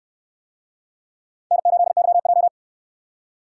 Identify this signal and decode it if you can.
Morse code, heard as a keyed tone, about 68 wpm, 700 Hz. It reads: N8YJ